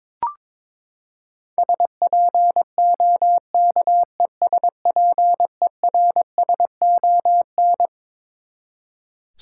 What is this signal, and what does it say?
Morse code, heard as a keyed tone, about 22 wpm, 700 Hz. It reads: SPOKESPERSON
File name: Twelve Letter Words - 22wpm